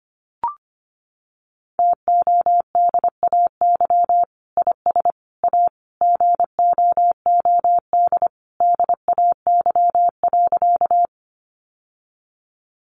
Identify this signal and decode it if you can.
Morse code, heard as a keyed tone, about 25 wpm, 700 Hz. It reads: TODAY IS A GOOD DAY.
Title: Sentences from Top 600 Words 25wpm